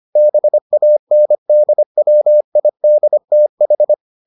Morse code, heard as a keyed tone, about 25 wpm, 600 Hz. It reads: BANDWIDTH